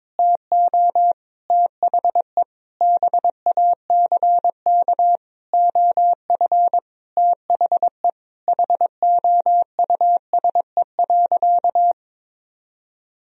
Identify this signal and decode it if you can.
Morse code, heard as a keyed tone, about 22 wpm, 700 Hz. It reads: TO THE BACK OF THE HOUSE.
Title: Sentences from Top 1000 Words - Review Part 1 - 22wpm